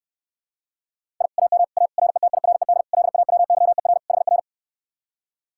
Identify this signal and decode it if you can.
Morse code, heard as a keyed tone, about 69 wpm, 700 Hz. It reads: I AM A RETIRED LAWYER HR